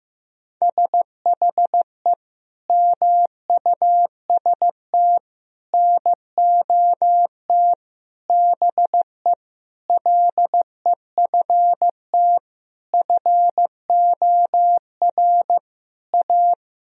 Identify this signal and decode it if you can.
Morse code, heard as a keyed tone, about 15 wpm, 700 Hz. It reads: SHE MUST NOT BE LEFT FOR A
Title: Sentences from Top 1000 Words - Review Part 1 - 15wpm